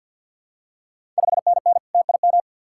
Morse code, heard as a keyed tone, about 50 wpm, 690 Hz. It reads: 5NN TIM
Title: Ontario QSO Party Exchange 50wpm